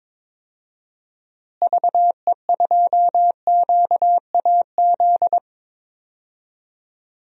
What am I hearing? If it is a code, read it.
Morse code, heard as a keyed tone, about 22 wpm, 700 Hz. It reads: VE2QAZ
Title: DXCC-100 Call Signs - 6 Characters - Repeated 2x - 22wpm